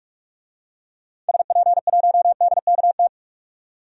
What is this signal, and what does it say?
Morse code, heard as a keyed tone, about 44 wpm, 700 Hz. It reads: SP1DKT
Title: DXCC-100 Call Signs - 6 Characters - 45wpm